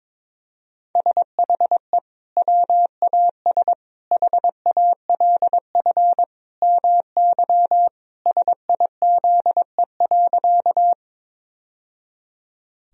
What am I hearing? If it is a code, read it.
Morse code, heard as a keyed tone, about 22 wpm, 700 Hz. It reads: SHE WAS HALF MY SIZE.
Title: Sentences from Top 1000 Words - Review Part 2 - 22wpm